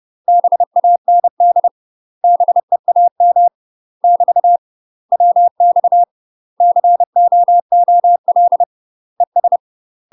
Morse code, heard as a keyed tone, about 30 wpm, 700 Hz. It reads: BAND BEAM <BT> WX COOL ES
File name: UK MASTER QSOs TRAINING / 30WPM / UK MASTER QSOs 0014